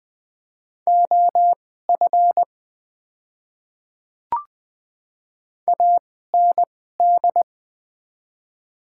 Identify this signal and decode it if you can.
Morse code, heard as a keyed tone, about 20 wpm, spaced as for 14 wpm, 690 Hz. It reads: OF AND
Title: 2x Character Spacing - Top 100 Words - 20wpm